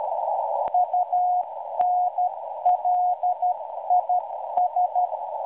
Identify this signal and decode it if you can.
Morse code, heard as a keyed tone, about 12 wpm, 720 Hz. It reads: UNLIS